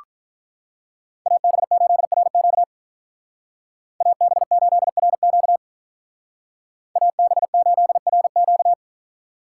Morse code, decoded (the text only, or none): AB8RQ AB8RQ AB8RQ